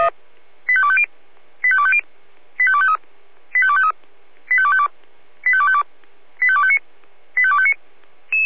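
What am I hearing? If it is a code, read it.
Five-tone selective call (ZVEI), heard as quick runs of five tones, tones of about 70 ms. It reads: 85250 85250 85253 85253 85252 85252 85259 85259
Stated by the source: ZVEI five-tone selective call received on an SDR.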